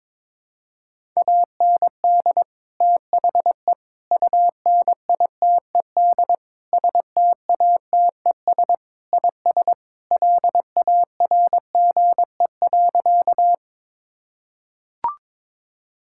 Morse code, heard as a keyed tone, about 21 wpm, 700 Hz. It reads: AND THE UNITED STATES IS LARGE.
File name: practice / 22wpm / Sentences with 4-Word Phrases - 22wpm